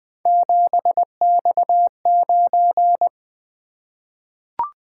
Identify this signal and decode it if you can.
Morse code, heard as a keyed tone, about 20 wpm, 700 Hz. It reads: 7X9
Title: Three Letter-Number Combinations - 20wpm